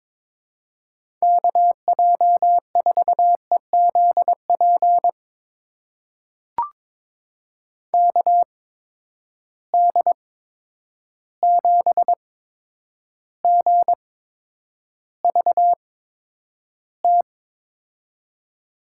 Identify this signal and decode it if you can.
Morse code, heard as a keyed tone, about 21 wpm, spaced as for 9 wpm, 700 Hz. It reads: KJ4EZP KD7GVT
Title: USA Call Signs - 6 Characters - 8x Character Spacing - 22wpm